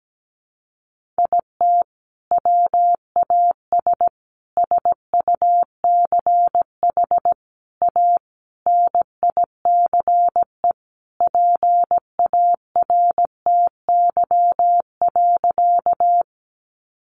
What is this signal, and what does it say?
Morse code, heard as a keyed tone, about 17 wpm, 700 Hz. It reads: IT WAS SUCH A NICE PARTY.
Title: Sentences from Top 1400 Words 17wpm